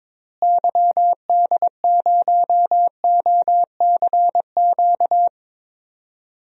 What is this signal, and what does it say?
Morse code, heard as a keyed tone, about 22 wpm, 700 Hz. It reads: YD0OCQ
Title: DXCC-100 Call Signs - 6 Characters - Repeated 3x - 22wpm